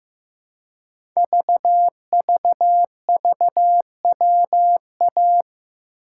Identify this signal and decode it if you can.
Morse code, heard as a keyed tone, about 15 wpm, 690 Hz. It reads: VVVWA
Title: Code Groups - 5 Letters - 15wpm